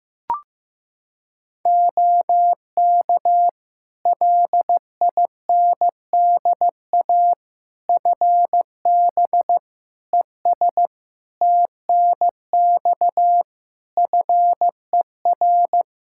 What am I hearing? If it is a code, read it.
Morse code, heard as a keyed tone, about 15 wpm, 700 Hz. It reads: OK LINDA FB ES TNX FER
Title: Practice QSO - Exchange 2 - Info - 15wpm